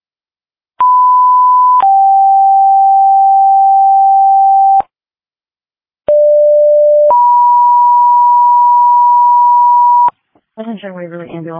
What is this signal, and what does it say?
Two-tone page, heard as a one-second tone, then a three-second tone, about 1000 then 780 Hz, about 590 then 970 Hz, with speech. It is heard from 0.8 to 10.1 s.
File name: Waverly Ambulance